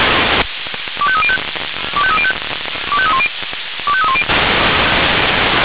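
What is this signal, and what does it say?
Five-tone selective call (ZVEI), heard as quick runs of five tones, tones of about 80 ms. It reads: 25305 25305 25310 25310
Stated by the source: ZVEI five-tone selective call received by SDR.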